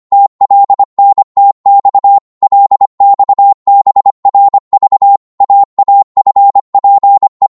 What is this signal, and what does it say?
Morse code, heard as a keyed tone, about 25 wpm, 800 Hz. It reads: TLNTX LXBRV AAFPE
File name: Morse / ltr-25wpm